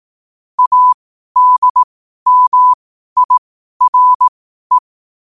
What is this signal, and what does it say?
Morse code, heard as a keyed tone, about 18 wpm, spaced as for 12 wpm, 1000 Hz. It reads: ADMIRE
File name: qotdfiles / 12wpm